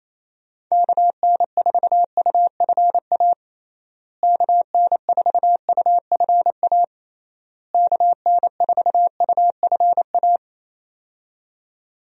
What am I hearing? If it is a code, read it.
Morse code, heard as a keyed tone, about 28 wpm, 700 Hz. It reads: KN4UFA KN4UFA KN4UFA
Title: USA Call Signs - 6 Characters - Repeated 3x - 28wpm